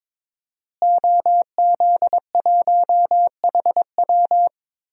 Morse code, heard as a keyed tone, about 22 wpm, 700 Hz. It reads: OZ1HW
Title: DXCC-100 Call Signs - 5 Characters - Repeated 3x - 22wpm